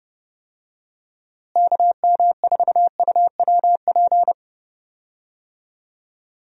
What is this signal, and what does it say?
Morse code, heard as a keyed tone, about 30 wpm, 700 Hz. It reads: KM4UWP
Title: USA Call Signs - 6 Characters - Repeated 3x - 30wpm